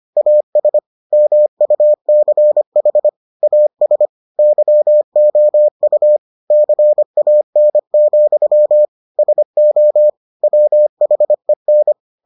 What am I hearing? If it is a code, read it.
Morse code, heard as a keyed tone, about 25 wpm, 600 Hz. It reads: AS MUCH AS YOU CAN, SO WHEN